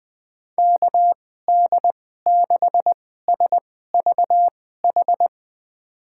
Morse code, heard as a keyed tone, about 20 wpm, spaced as for 14 wpm, 700 Hz. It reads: KD6SVH
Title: USA Call Signs - 6 Characters - 2x Character Spacing - 20wpm